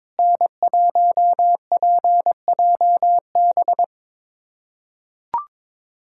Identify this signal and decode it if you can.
Morse code, heard as a keyed tone, about 21 wpm, 700 Hz. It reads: N1PJB
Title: USA Callsign DE Callsign - 5 Characters - 22wpm